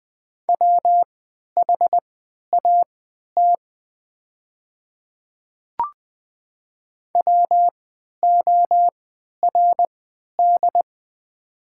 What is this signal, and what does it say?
Morse code, heard as a keyed tone, about 20 wpm, spaced as for 11 wpm, 700 Hz. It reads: WHAT WORD
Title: 3x Character Spacing - Top 100 Words - 20wpm